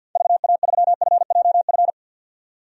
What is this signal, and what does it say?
Morse code, heard as a keyed tone, about 46 wpm, 700 Hz. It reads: VA3FJF
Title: DXCC-100 Call Signs - 6 Characters - 50wpm